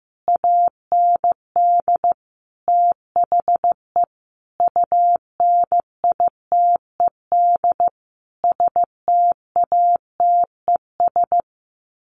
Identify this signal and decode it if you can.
Morse code, heard as a keyed tone, about 15 wpm, 700 Hz. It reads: AND THE UNITED STATES
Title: Sets of 5 Words Encore2 15wpm